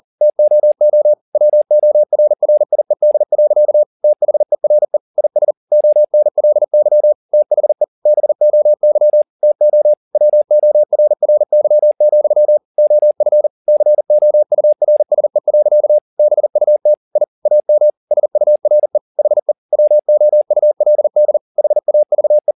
Morse code, heard as a keyed tone, about 40 wpm, 600 Hz. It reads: TOO WORRIED. THERE IS ONLY THE BOY TO WORRY, OF COURSE. BUT I AM SURE HE WOULD HAVE
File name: the old man and the sea 2 40wpm